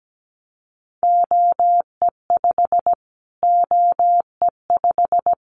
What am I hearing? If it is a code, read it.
Morse code, heard as a keyed tone, about 17 wpm, 700 Hz. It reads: OE5 OE5
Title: Callsign Prefixes for Top 25 DXCC - Repeated 3x - 17wpm